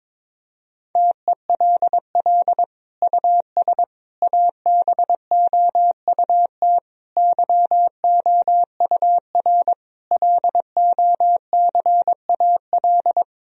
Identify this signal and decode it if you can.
Morse code, heard as a keyed tone, about 22 wpm, 700 Hz. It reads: TELL US ABOUT YOUR LOCAL
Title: Sentences from Top 1000 Words - Review Part 3 - 22wpm